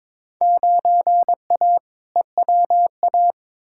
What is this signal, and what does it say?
Morse code, heard as a keyed tone, about 22 wpm, 700 Hz. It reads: 9A EWA
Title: Field Day Exchanges Encore - 22wpm